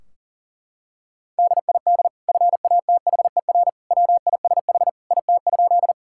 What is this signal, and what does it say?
Morse code, heard as a keyed tone, about 40 wpm, 700 Hz. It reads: DID FATHER WISH IT?
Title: Sentences from Top 1600 Words 40wpm